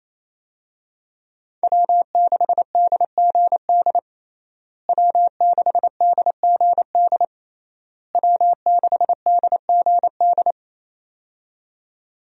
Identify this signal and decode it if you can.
Morse code, heard as a keyed tone, about 28 wpm, 700 Hz. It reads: W6DGD W6DGD W6DGD
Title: USA Call Signs - 5 Characters - Repeated 3x - 28wpm